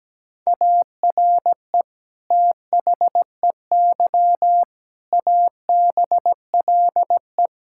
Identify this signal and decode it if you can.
Morse code, heard as a keyed tone, about 17 wpm, 700 Hz. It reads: ARE THEY ABLE
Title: Sentences from Top 1000 Words - Review Part 5 - 17wpm